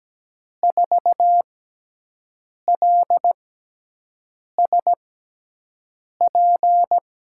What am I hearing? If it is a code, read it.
Morse code, heard as a keyed tone, about 17 wpm, spaced as for 6 wpm, 700 Hz. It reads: 4LSP
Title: USA Call Signs - 5 Characters - 6x Character Spacing - 17wpm